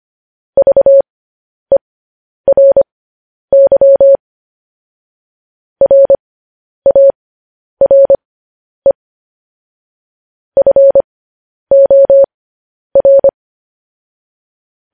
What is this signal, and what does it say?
Morse code, heard as a keyed tone, about 25 wpm, spaced as for 10 wpm, 550 Hz. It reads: VERY RARE FOR
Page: news-headlines-in-morse-code-at-10-wpm